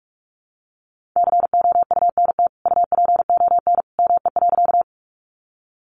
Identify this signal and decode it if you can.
Morse code, heard as a keyed tone, about 44 wpm, 700 Hz. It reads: COUNT UPON ME.
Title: Sentences from Top 1500 Words 45wpm